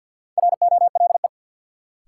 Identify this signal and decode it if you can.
Morse code, heard as a keyed tone, about 48 wpm, 700 Hz. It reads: ROLE
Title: Sets of 3 Words - New Word Drill - 50wpm